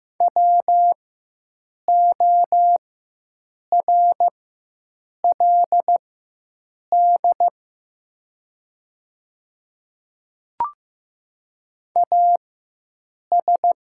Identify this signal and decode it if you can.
Morse code, heard as a keyed tone, about 15 wpm, spaced as for 7 wpm, 700 Hz. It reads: WORLD AS
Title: 4x Character Spacing - Top 200 Words - 15wpm